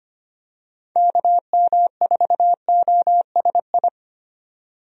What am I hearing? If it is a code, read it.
Morse code, heard as a keyed tone, about 25 wpm, 700 Hz. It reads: KM4OSI
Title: USA Call Signs - 6 Characters - 25wpm